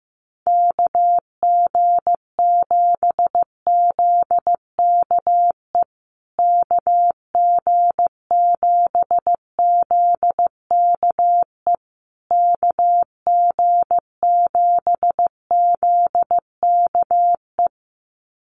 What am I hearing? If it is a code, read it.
Morse code, heard as a keyed tone, about 15 wpm, 700 Hz. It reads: KG7ZKE KG7ZKE KG7ZKE
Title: USA Call Signs Repeated 3x 15wpm